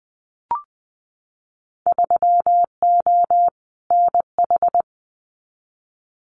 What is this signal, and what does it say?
Morse code, heard as a keyed tone, about 20 wpm, 700 Hz. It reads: 3O NH
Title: Winter Field Day Exchanges Encore 20wpm